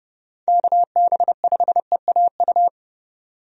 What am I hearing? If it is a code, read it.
Morse code, heard as a keyed tone, about 31 wpm, 700 Hz. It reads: KB5EAU
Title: USA Call Signs - 6 Characters - 30wpm